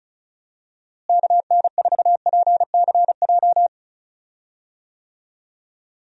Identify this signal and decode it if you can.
Morse code, heard as a keyed tone, about 36 wpm, 700 Hz. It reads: KN4PCJ
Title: USA Call Signs - 6 Characters - Repeated 3x - 35wpm